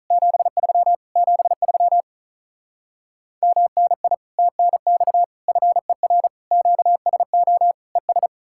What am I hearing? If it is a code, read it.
Morse code, heard as a keyed tone, about 47 wpm, 700 Hz. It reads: 73 73 MNI TNX FER QSO ES